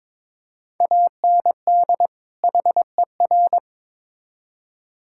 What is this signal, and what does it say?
Morse code, heard as a keyed tone, about 22 wpm, 700 Hz. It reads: AND HER
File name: Build A Phrase - 2-Word Phrases - 22wpm